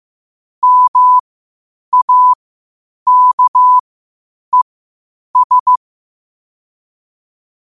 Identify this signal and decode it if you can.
Morse code, heard as a keyed tone, about 15 wpm, spaced as for 8 wpm, 1000 Hz. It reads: MAKES
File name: qotdfiles / 8wpm